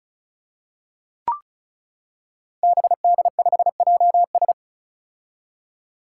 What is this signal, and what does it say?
Morse code, heard as a keyed tone, about 36 wpm, 700 Hz. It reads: BD5JS
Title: Five Challenging Letter-Number Combinations 35wpm